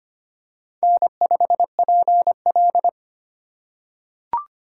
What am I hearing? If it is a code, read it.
Morse code, heard as a keyed tone, about 25 wpm, 700 Hz. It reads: N5PL
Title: USA Call Signs - 4 Characters - 2x Character Spacing - 25wpm